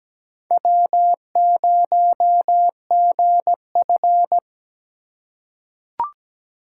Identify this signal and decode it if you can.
Morse code, heard as a keyed tone, about 17 wpm, 700 Hz. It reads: W0GF
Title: USA Call Signs - 4 Characters - 2x Character Spacing - 17wpm